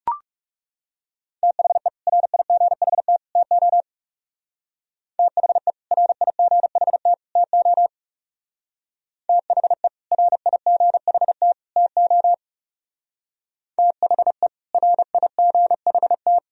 Morse code, heard as a keyed tone, about 37 wpm, 700 Hz. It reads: THE RIGHT TO THE RIGHT TO THE RIGHT TO THE RIGHT
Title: Speed-Racing - Sets of 3 Words 30wpm